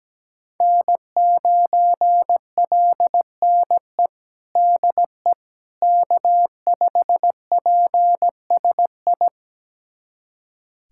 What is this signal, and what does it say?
Morse code, heard as a keyed tone, about 17 wpm, 700 Hz. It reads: N9LNE DE K5PSI
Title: USA Callsign DE Callsign - 5 Characters - 17wpm